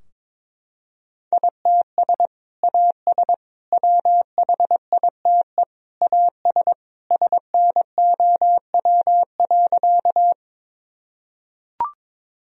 Morse code, heard as a keyed tone, about 21 wpm, 700 Hz. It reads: ITS AS WHITE AS SNOW.
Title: Sentences from Top 1000 Words - Review Part 3 - 22wpm